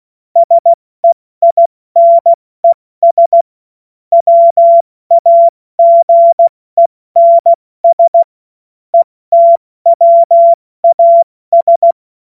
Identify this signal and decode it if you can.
Morse code, heard as a keyed tone, about 16 wpm, 680 Hz. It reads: SEINES WAGENS ETWAS
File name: Morsekurs / Witze / Witz 02. August 2022 70 F80